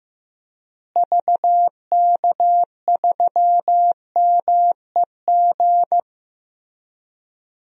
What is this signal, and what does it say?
Morse code, heard as a keyed tone, about 14 wpm, 700 Hz. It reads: VK3MEG
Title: DXCC-100 Call Signs - 6 Characters - Repeated 3x - 15wpm